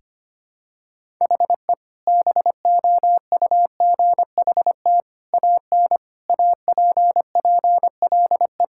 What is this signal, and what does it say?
Morse code, heard as a keyed tone, about 25 wpm, 700 Hz. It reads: HE BOUGHT AN APPLE
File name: practice / 25wpm / Sentences with 2-Word Phrases - 25wpm